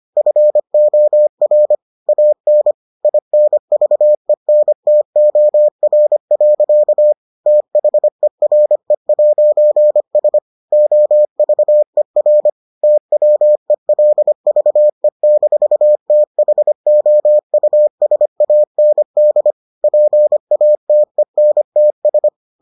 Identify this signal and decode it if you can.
Morse code, heard as a keyed tone, about 25 wpm, 600 Hz. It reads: FOR AN INVENTOR. THERE'S OVER TWELVE-THOUSAND PATENTS